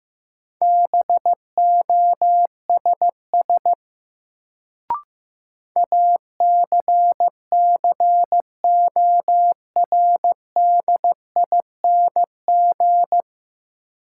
Morse code, heard as a keyed tone, about 15 wpm, 700 Hz. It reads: BOSS ACCORDING
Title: Sentences from Top 700 Words - New Word Drill - 15wpm